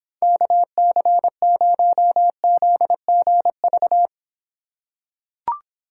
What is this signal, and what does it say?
Morse code, heard as a keyed tone, about 26 wpm, 700 Hz. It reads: KC0ZGV